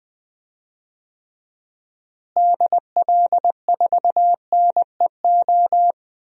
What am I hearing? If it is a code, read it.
Morse code, heard as a keyed tone, about 20 wpm, 700 Hz. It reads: DL4NEO
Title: DXCC-100 Call Signs - 6 Characters - Repeated 2x - 20wpm